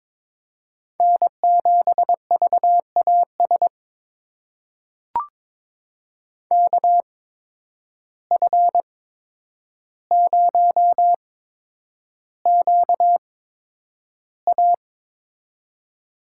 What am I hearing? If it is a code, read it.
Morse code, heard as a keyed tone, about 22 wpm, spaced as for 9 wpm, 700 Hz. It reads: N7VAS KF0QA
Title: USA Call Signs - 5 Characters - 8x Character Spacing - 22wpm